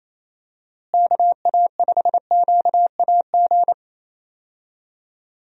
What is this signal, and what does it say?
Morse code, heard as a keyed tone, about 28 wpm, 700 Hz. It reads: KA5QAG
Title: USA Call Signs - 6 Characters - Repeated 3x - 28wpm